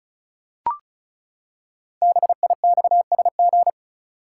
Morse code, heard as a keyed tone, about 36 wpm, 700 Hz. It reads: BIXSG
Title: Code Groups - 5 Letters - 35wpm